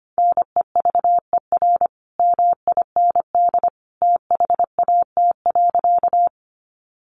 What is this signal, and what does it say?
Morse code, heard as a keyed tone, about 25 wpm, 700 Hz. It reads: NEVER MIND THAT.
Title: Sentences from Top 500 Words 25wpm